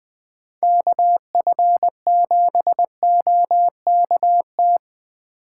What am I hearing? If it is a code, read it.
Morse code, heard as a keyed tone, about 19 wpm, 700 Hz. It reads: KF7OKT
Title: Speed Racing - USA Call Signs 17wpm